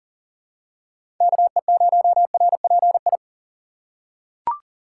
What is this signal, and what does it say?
Morse code, heard as a keyed tone, about 40 wpm, 700 Hz. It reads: KE0RPI
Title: USA Call Signs - 6 Characters - 3x Character Spacing - 40wpm